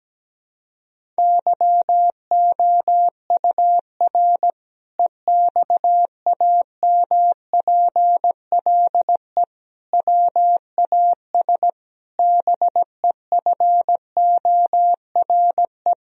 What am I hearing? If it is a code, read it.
Morse code, heard as a keyed tone, about 17 wpm, 700 Hz. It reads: YOUR EXAMPLE WAS BEFORE
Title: Sentences from Top 1000 Words - Review Part 1 - 17wpm